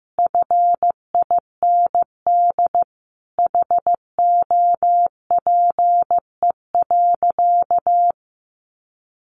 Morse code, heard as a keyed tone, about 15 wpm, 700 Hz. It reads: FIND HOPE.
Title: Sentences from Top 1300 Words 15wpm